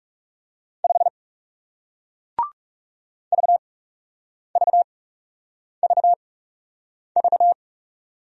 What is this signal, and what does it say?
Morse code, heard as a keyed tone, about 40 wpm, spaced as for 8 wpm, 700 Hz. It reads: 5 VVVV